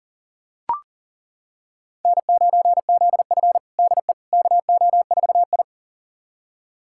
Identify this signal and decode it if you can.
Morse code, heard as a keyed tone, about 40 wpm, 700 Hz. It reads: N9ZF DE KO4I
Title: USA Callsign DE Callsign - 4 Characters - 40wpm